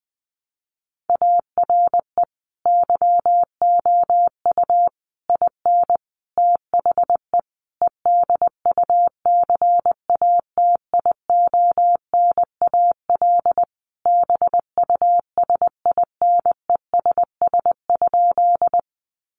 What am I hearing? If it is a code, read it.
Morse code, heard as a keyed tone, about 20 wpm, 700 Hz. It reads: ARE YOU IN THE EDUCATIONAL BUSINESS?
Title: Sentences from Top 1200 Words 20wpm